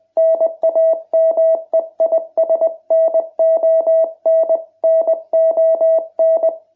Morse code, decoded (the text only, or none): NAME IS NON NON